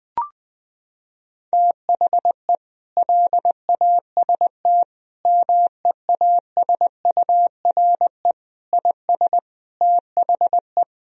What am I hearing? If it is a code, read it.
Morse code, heard as a keyed tone, about 21 wpm, 700 Hz. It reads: THE LAST MEASURE IS THE
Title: Sentences from Top 1000 Words - Review Part 2 - 20wpm